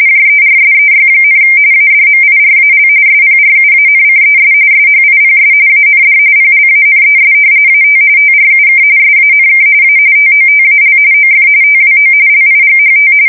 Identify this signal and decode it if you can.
RTTY, about 45.45 baud. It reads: JERSEY, ADMIRAL SPRUANCES SHIPS AND PLANES AT TRUK SANK A TOTAL OF TWO LIGHT
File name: fsk170 / battleshipnj